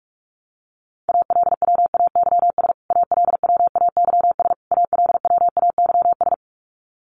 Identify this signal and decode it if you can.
Morse code, heard as a keyed tone, about 44 wpm, 700 Hz. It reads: ALWAYS ALWAYS ALWAYS
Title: Top 500 Words Repeated 45wpm